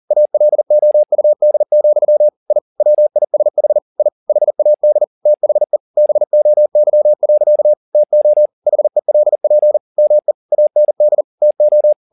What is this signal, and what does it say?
Morse code, heard as a keyed tone, about 40 wpm, 600 Hz. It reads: ALOUD, I WISH I HAD THE BOY. TO HELP ME AND TO